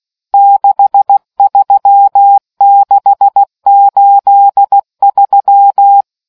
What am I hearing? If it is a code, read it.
Morse code, heard as a keyed tone, about 16 wpm, 800 Hz. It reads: 63683